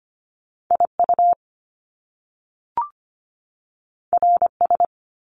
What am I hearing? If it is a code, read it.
Morse code, heard as a keyed tone, about 25 wpm, 700 Hz. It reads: IU RS
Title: Top Two Letter Combinations in Words 25wpm